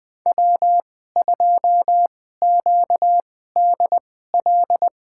Morse code, heard as a keyed tone, about 21 wpm, spaced as for 15 wpm, 700 Hz. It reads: W2QDL
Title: USA Call Signs - 5 Characters - 2x Character Spacing - 20wpm